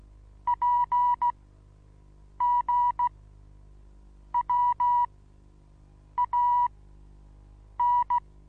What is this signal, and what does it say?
Morse code, heard as a keyed tone, about 16 wpm, spaced as for 6 wpm, 970 Hz. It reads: PGWAN
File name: Onlinelehrgang / morsen / DJ4UF Morsen25